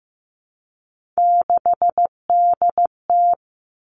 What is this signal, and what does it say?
Morse code, heard as a keyed tone, about 15 wpm, 700 Hz. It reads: 6DT
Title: Three Challenging Letter-Number Combinations 15wpm